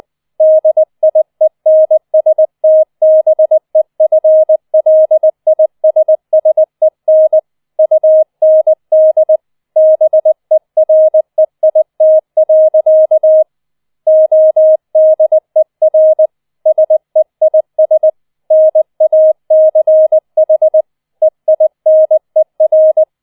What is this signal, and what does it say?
Morse code, heard as a keyed tone, about 20 wpm, 630 Hz. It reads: DIENSTBEFLISSEN UND BEREIT. ODER SEIS NACH EINER